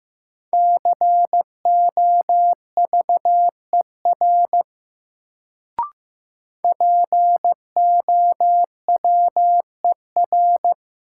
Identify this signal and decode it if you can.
Morse code, heard as a keyed tone, about 15 wpm, 700 Hz. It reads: COVER POWER
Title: Sentences from Top 500 Words - New Word Drill - 15wpm